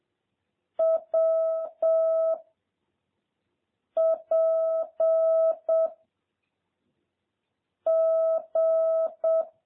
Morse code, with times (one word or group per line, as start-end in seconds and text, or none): 0.8-2.3 W
4.0-5.9 P
7.9-9.4 G